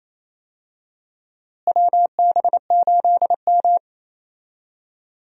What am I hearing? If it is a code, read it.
Morse code, heard as a keyed tone, about 28 wpm, 700 Hz. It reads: WB8M
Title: USA Call Signs - 4 Characters - Repeated 3x - 28wpm